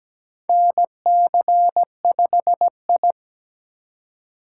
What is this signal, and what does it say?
Morse code, heard as a keyed tone, about 17 wpm, 700 Hz. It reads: NC5I
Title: USA Call Signs - 4x Character Spacing - 17wpm